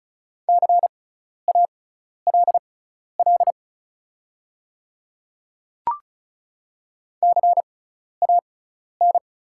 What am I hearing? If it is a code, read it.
Morse code, heard as a keyed tone, about 34 wpm, spaced as for 12 wpm, 690 Hz. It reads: CALL CAN
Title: 6x Character Spacing - Top 100 Words - 35wpm